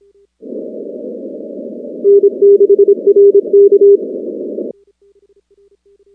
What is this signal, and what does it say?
Morse code, heard as a keyed tone, about 26 wpm, 400 Hz. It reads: N6RK
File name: contests / 2019-cw-wpx-kt5j-k5pi- -2330 0-19